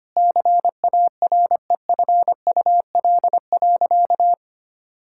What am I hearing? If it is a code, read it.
Morse code, heard as a keyed tone, about 25 wpm, 700 Hz. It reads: CAREFUL.
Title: Sentences from Top 1600 Words 25wpm